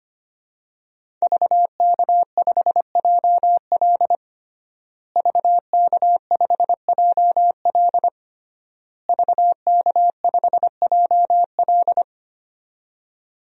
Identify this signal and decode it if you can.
Morse code, heard as a keyed tone, about 25 wpm, 700 Hz. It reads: VK5JL VK5JL VK5JL
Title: DXCC-100 Call Signs - 5 Characters - Repeated 3x - 25wpm